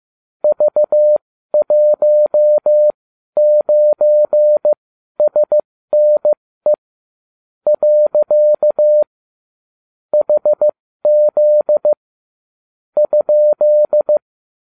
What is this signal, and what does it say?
Morse code, heard as a keyed tone, about 15 wpm, spaced as for 12 wpm, 600 Hz. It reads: V19SNE . HZ ?